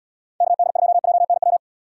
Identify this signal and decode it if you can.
Morse code, heard as a keyed tone, about 75 wpm, 700 Hz. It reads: KD2JNW